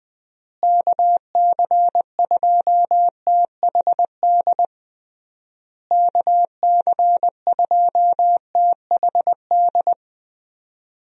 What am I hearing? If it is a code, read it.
Morse code, heard as a keyed tone, about 20 wpm, 700 Hz. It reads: KC2THD KC2THD
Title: USA Call Signs - 6 Characters - Repeated 3x - 20wpm